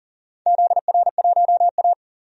Morse code, heard as a keyed tone, about 40 wpm, 700 Hz. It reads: ZR1A
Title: DXCC-100 Call Signs - 4 Characters - 40wpm